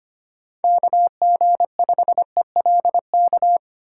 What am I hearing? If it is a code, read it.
Morse code, heard as a keyed tone, about 25 wpm, 700 Hz. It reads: KG5ELK
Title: USA Call Signs - 6 Characters - 6x Character Spacing - 25wpm